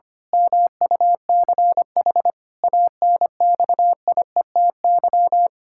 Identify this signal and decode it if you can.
Morse code, heard as a keyed tone, about 25 wpm, 700 Hz. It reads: MUCH ANXIETY
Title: Pangram Challenge v2 - 25wpm